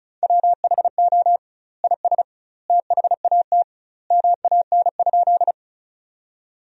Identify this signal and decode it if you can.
Morse code, heard as a keyed tone, about 36 wpm, 700 Hz. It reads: WHO IS THAT MAN?
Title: 2x Word-Spacing - Sentences from Top 200 Words - 35wpm